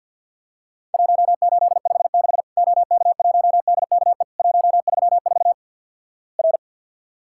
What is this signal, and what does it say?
Morse code, heard as a keyed tone, about 53 wpm, 700 Hz. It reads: 1856 OK1DKE 134 R
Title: Worked All Europe DX Contest - QTC Exchange with Repeats - 50wpm